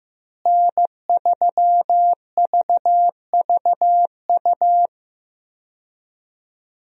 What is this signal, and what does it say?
Morse code, heard as a keyed tone, about 15 wpm, 700 Hz. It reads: N3VVU
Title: USA Call Signs - 5 Characters - Repeated 3x - 15wpm